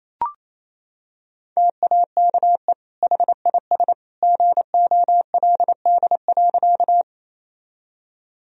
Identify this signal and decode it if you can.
Morse code, heard as a keyed tone, about 28 wpm, 700 Hz. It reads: TAKE HIS GOLD.
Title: Sentences from Top 1000 Words - Review Part 3 - 28wpm